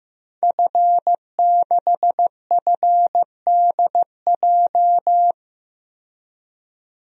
Morse code, heard as a keyed tone, about 15 wpm, 700 Hz. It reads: F6FDJ
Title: DXCC-100 Call Signs - 5 Characters - 15wpm